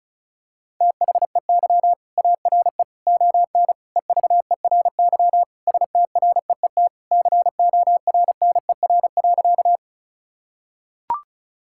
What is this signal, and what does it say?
Morse code, heard as a keyed tone, about 36 wpm, 700 Hz. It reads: THEY ARE ON EVERY STREET CORNER.
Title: Sentences from Top 1600 Words 35wpm